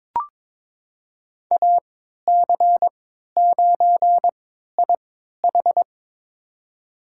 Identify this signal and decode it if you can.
Morse code, heard as a keyed tone, about 22 wpm, spaced as for 13 wpm, 700 Hz. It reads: AC9IH
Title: USA Call Signs - 5 Characters - 3x Character Spacing - 22wpm